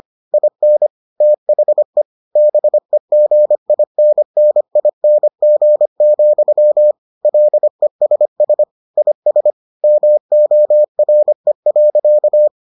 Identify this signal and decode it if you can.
Morse code, heard as a keyed tone, about 25 wpm, 600 Hz. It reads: IN THE BEGINNING, LESS IS MORE.